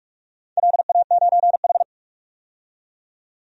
Morse code, heard as a keyed tone, about 44 wpm, 700 Hz. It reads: LA9H
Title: DXCC-100 Call Signs - 4 Characters - 45wpm